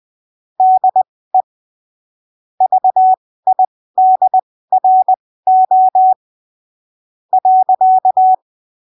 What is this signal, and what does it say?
Morse code, heard as a keyed tone, about 20 wpm, spaced as for 15 wpm, 750 Hz. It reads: DE VIDRO .
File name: arquivos / texto / texto 11 20ppm